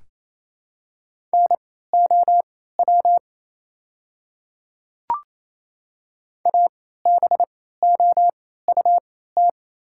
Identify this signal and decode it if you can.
Morse code, heard as a keyed tone, about 27 wpm, spaced as for 16 wpm, 700 Hz. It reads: NOW ABOUT
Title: 3x Character Spacing - Top 100 Words - 28wpm